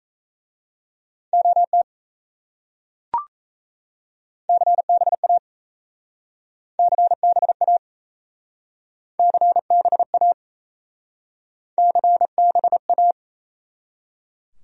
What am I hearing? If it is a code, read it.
Morse code, heard as a keyed tone, about 36 wpm, 700 Hz. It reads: OT CBA CBA CBA CBA